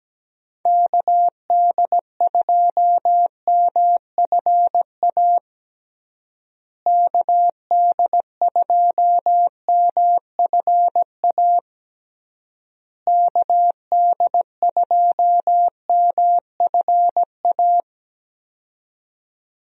Morse code, heard as a keyed tone, about 17 wpm, 700 Hz. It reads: KD2MFA KD2MFA KD2MFA
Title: USA Call Signs - 6 Characters - Repeated 3x - 17wpm